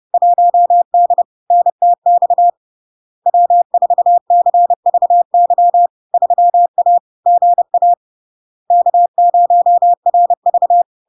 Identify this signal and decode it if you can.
Morse code, heard as a keyed tone, about 30 wpm, 700 Hz. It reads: 1D NTX W4CVY 3A GA K0RV
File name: LARK-FD-2022-Log-in-CW-30wpm-700-Hz